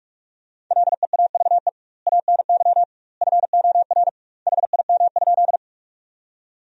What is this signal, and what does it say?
Morse code, heard as a keyed tone, about 46 wpm, 700 Hz. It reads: LEAVE ANY FOR HIM?
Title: 2x Word-Spacing - Sentences from Top 400 Words - 45wpm